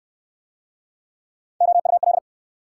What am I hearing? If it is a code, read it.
Morse code, heard as a keyed tone, about 59 wpm, 700 Hz. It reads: OUL